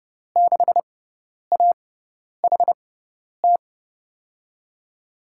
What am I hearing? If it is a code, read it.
Morse code, heard as a keyed tone, about 30 wpm, spaced as for 10 wpm, 700 Hz. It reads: 6AHT
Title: USA Call Signs - 5 Characters - 6x Character Spacing - 30wpm